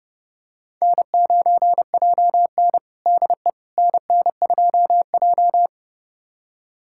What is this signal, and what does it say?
Morse code, heard as a keyed tone, about 30 wpm, 700 Hz. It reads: N9JN DE NN2J
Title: USA Callsign DE Callsign - 4 Characters - 30wpm